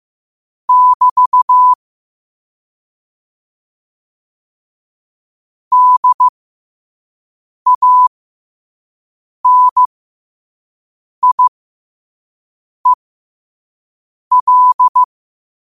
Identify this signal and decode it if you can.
Morse code, heard as a keyed tone, about 15 wpm, spaced as for 5 wpm, 1000 Hz. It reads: <BT> DANIEL
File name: qotdfiles / 5wpm